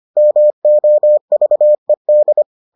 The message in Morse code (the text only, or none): MOVED